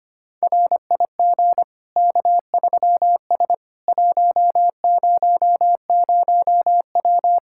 Morse code, heard as a keyed tone, about 25 wpm, 700 Hz. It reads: RIG K3S 100W
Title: K1USN SST QSO Practice 25wpm